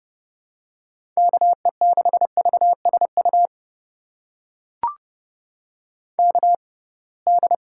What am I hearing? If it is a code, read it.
Morse code, heard as a keyed tone, about 32 wpm, spaced as for 23 wpm, 700 Hz. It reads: KE6VSU KD
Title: USA Call Signs - 6 Characters - 6x Character Spacing - 30wpm